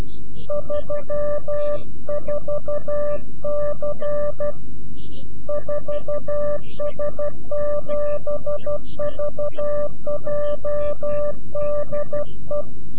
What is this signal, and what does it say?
Morse code, heard as a keyed tone, about 12 wpm, 590 Hz. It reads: 34C 4S7VJ DE
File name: CW-QSO / CQ Call / CQ-QSO